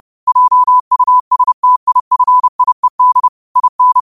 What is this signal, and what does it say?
Morse code, heard as a keyed tone, about 30 wpm, 1000 Hz. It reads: JUSTIFIED IN
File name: qotdfiles / 30wpm